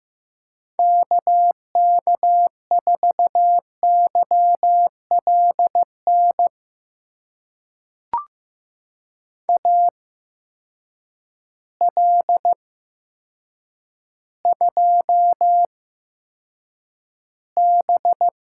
Morse code, heard as a keyed tone, about 15 wpm, spaced as for 7 wpm, 700 Hz. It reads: KK4YLN AL2B
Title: USA Call Signs - 8x Character Spacing - 15wpm